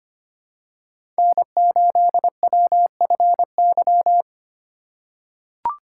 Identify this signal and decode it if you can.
Morse code, heard as a keyed tone, about 25 wpm, 700 Hz. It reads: N8WFY
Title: USA Call Signs - 5 Characters - 4x Character Spacing - 25wpm